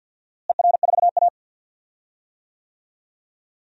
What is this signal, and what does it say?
Morse code, heard as a keyed tone, about 51 wpm, 700 Hz. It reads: ER4A
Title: DXCC-100 Call Signs - 4 Characters - Repeated 2x - 50wpm